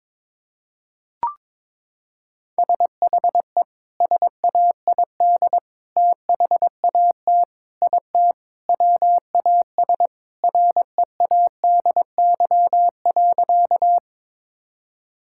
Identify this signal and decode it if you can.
Morse code, heard as a keyed tone, about 21 wpm, 700 Hz. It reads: SHE SAID THAT IT WAS READY.
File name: practice / 22wpm / Sentences with 3-Word Phrases - 22wpm